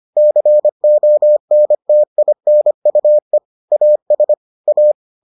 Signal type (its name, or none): Morse code